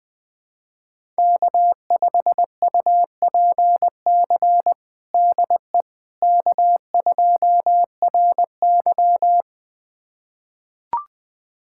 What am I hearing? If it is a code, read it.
Morse code, heard as a keyed tone, about 20 wpm, 700 Hz. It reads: K5UPC DE K2RY
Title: USA Callsign DE Callsign 20wpm